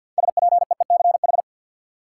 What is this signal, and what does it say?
Morse code, heard as a keyed tone, about 52 wpm, 700 Hz. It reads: SPEECH
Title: Six Letter Words - 50wpm